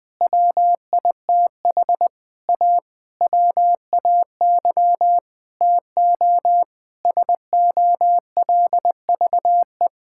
Morse code, heard as a keyed tone, about 19 wpm, 700 Hz. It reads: WITH A WAY TO SOLVE
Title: Sentences with 7-Word Phrases - 20wpm